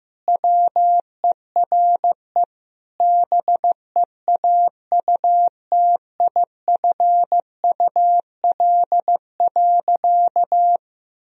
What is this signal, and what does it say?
Morse code, heard as a keyed tone, about 15 wpm, 700 Hz. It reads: WERE BEAUTIFUL.
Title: Sentences from Top 1000 Words - Review Part 2 - 15wpm